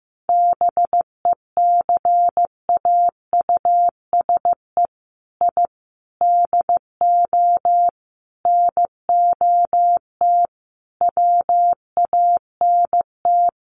Morse code, heard as a keyed tone, about 15 wpm, 700 Hz. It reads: BECAUSE I DO NOT WANT
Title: Sets of 5 Words Encore 15wpm